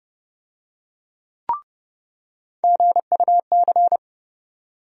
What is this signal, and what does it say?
Morse code, heard as a keyed tone, about 30 wpm, 700 Hz. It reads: GUC
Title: USA Call Sign Postfixes - 3 Letters 30wpm